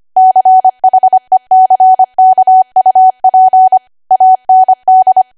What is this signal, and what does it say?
Morse code, heard as a keyed tone, about 25 wpm, 750 Hz. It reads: CHECKUP AND
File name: Archive / 25 WPM / 25WPM